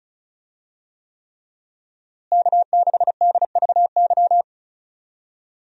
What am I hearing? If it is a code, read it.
Morse code, heard as a keyed tone, about 34 wpm, 700 Hz. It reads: K6DVY
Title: USA Call Signs - 5 Characters - Repeated 3x - 35wpm